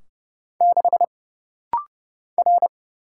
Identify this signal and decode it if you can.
Morse code, heard as a keyed tone, about 31 wpm, 700 Hz. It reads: 6 R
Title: Single Letter-Number Alt - Rapid-Fire 30wpm